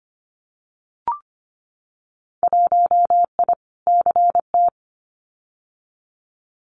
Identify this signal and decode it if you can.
Morse code, heard as a keyed tone, about 25 wpm, 700 Hz. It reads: 1I CT
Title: Winter Field Day Exchanges 25wpm